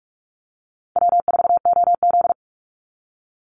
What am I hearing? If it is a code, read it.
Morse code, heard as a keyed tone, about 44 wpm, 700 Hz. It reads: W4OZ
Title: USA Call Signs 45wpm